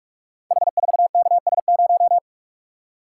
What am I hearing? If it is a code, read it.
Morse code, heard as a keyed tone, about 44 wpm, 700 Hz. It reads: H4KS0
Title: Code Groups - 5 Letter:Numbers - 45wpm